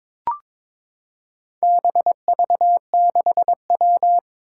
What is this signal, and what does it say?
Morse code, heard as a keyed tone, about 23 wpm, 700 Hz. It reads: BV6W
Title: Four Challenging Letter-Number Combinations 22wpm